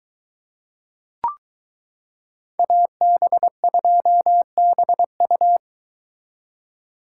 Morse code, heard as a keyed tone, about 22 wpm, 700 Hz. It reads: AB2BU